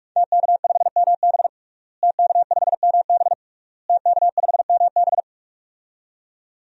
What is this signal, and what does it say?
Morse code, heard as a keyed tone, about 46 wpm, 700 Hz. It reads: TK5MB TK5MB TK5MB
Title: DXCC-100 Call Signs - 5 Characters - Repeated 3x - 45wpm